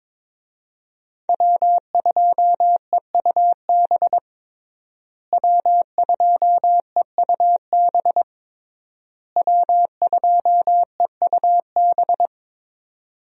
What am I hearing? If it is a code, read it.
Morse code, heard as a keyed tone, about 22 wpm, 700 Hz. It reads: W2EUB W2EUB W2EUB
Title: USA Call Signs - 5 Characters - Repeated 3x - 22wpm